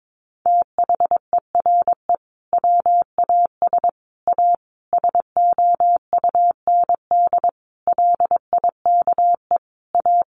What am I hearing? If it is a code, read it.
Morse code, heard as a keyed tone, about 22 wpm, 700 Hz. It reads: THERE WAS A SOUND LIKE A
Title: A Sound Of Thunder 22wpm